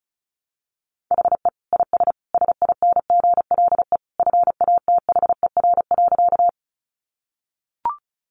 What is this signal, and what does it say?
Morse code, heard as a keyed tone, about 34 wpm, 700 Hz. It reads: HE IS SINGLE FATHER.
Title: Sentences from Top 700 Words 35wpm